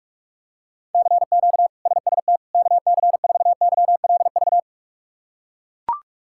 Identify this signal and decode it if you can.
Morse code, heard as a keyed tone, about 42 wpm, 700 Hz. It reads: CQ SST KC4YLV
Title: K1USN SST QSO Practice - 45wpm